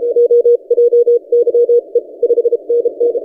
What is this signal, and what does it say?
Morse code, heard as a keyed tone, about 30 wpm, 480 Hz. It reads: OJYE 5NN
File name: posts / vk9ec-40cw- -1410z